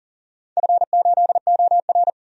Morse code, heard as a keyed tone, about 39 wpm, 690 Hz. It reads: F8OR
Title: DXCC-100 Call Signs - 4 Characters - 40wpm